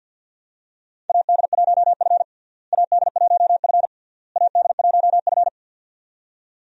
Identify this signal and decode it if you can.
Morse code, heard as a keyed tone, about 51 wpm, 700 Hz. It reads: AD1F AD1F AD1F
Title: USA Call Signs - 4 Characters - Repeated 3x - 50wpm